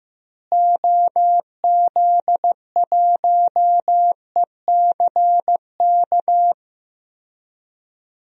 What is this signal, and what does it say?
Morse code, heard as a keyed tone, about 16 wpm, 700 Hz. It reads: OZ1ECK
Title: DXCC-100 Call Signs - 6 Characters - Repeated 3x - 15wpm